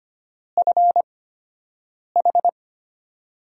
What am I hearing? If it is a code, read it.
Morse code, heard as a keyed tone, about 25 wpm, spaced as for 7 wpm, 700 Hz. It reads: FH
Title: USA Call Signs - 4 Characters - 8x Character Spacing - 25wpm